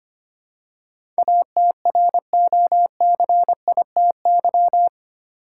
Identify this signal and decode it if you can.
Morse code, heard as a keyed tone, about 25 wpm, 700 Hz. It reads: ATROCITY
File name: Eight Letter Words - 25wpm